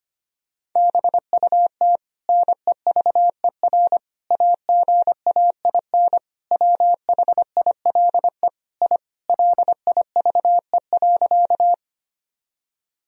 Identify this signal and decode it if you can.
Morse code, heard as a keyed tone, about 25 wpm, 700 Hz. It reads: BUT NEVER AGAIN WHILE I LIVE.
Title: Sentences from Top 1000 Words - Review Part 1 - 25wpm